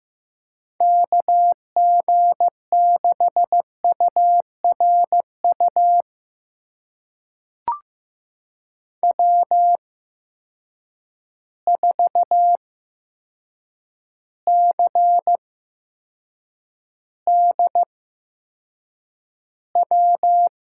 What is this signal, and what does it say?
Morse code, heard as a keyed tone, about 15 wpm, spaced as for 7 wpm, 700 Hz. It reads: KG6URU W4CDW
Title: USA Call Signs - 8x Character Spacing - 15wpm